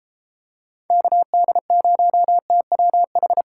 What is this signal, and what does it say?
Morse code, heard as a keyed tone, about 32 wpm, 700 Hz. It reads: KD0TWH